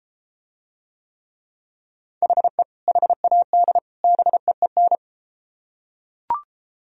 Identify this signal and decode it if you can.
Morse code, heard as a keyed tone, about 32 wpm, 700 Hz. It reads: HE HAD BEEN